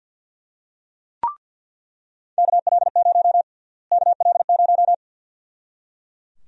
Morse code, heard as a keyed tone, about 46 wpm, 700 Hz. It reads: KL0 KL0
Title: USA Call Sign Prefixes - Repeated 2x - 50wpm